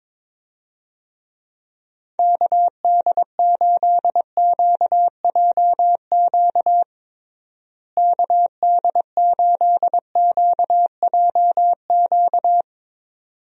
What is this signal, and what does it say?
Morse code, heard as a keyed tone, about 21 wpm, 700 Hz. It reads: KD8QJQ KD8QJQ
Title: USA Call Signs - 6 Characters - Repeated 2x - 22wpm